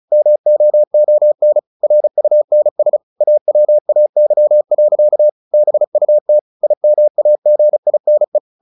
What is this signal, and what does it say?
Morse code, heard as a keyed tone, about 34 wpm, 600 Hz. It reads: MOON RUNS AWAY. BUT IMAGINE
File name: the old man and the sea 2 35wpm